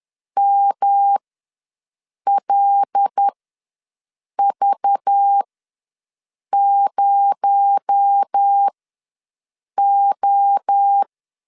Morse code, heard as a keyed tone, about 10 wpm, spaced as for 6 wpm, 800 Hz. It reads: MLV0O
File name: LEZIONI / 6 PKX0 / 6 PKX0 ws 10wpm 4220